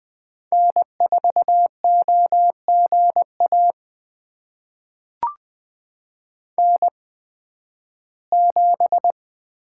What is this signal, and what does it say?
Morse code, heard as a keyed tone, about 20 wpm, spaced as for 13 wpm, 700 Hz. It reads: N4OGA N7
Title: USA Call Signs - 5 Characters - 8x Character Spacing - 20wpm